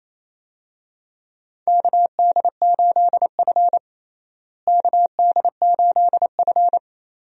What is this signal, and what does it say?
Morse code, heard as a keyed tone, about 28 wpm, 700 Hz. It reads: KD8F KD8F
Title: USA Call Signs - 4 Characters - Repeated 2x - 28wpm